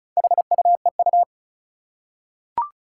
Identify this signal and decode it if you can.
Morse code, heard as a keyed tone, about 36 wpm, 700 Hz. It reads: HUEU
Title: Four Challenging Letter-Number Combinations 35wpm